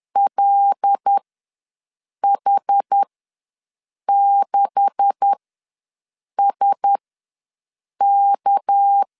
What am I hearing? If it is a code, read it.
Morse code, heard as a keyed tone, about 10 wpm, spaced as for 6 wpm, 800 Hz. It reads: LH6SK
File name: LEZIONI / 10 SIGNS / 10A SLASH ws 10wpm 4121